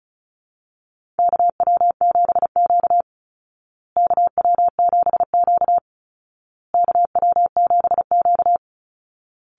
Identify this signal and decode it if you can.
Morse code, heard as a keyed tone, about 34 wpm, 700 Hz. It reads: KW7Q KW7Q KW7Q
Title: Call Signs Repeated 2x - 35wpm